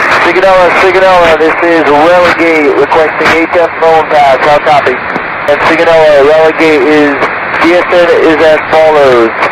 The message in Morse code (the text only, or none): none